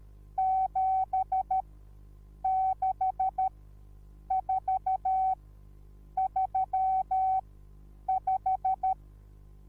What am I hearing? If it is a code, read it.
Morse code, heard as a keyed tone, about 13 wpm, spaced as for 8 wpm, 760 Hz. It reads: 76435
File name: Onlinelehrgang / morsen / DJ4UF Morsen21